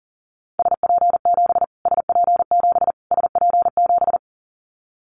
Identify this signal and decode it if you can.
Morse code, heard as a keyed tone, about 40 wpm, 700 Hz. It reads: SP7 SP7 SP7
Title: Callsign Prefixes for Top 25 DXCC - Repeated 3x - 40wpm